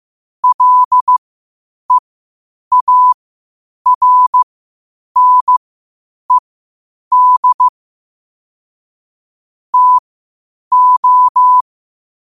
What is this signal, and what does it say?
Morse code, heard as a keyed tone, about 15 wpm, spaced as for 8 wpm, 1000 Hz. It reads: LEARNED TO
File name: qotdfiles / 8wpm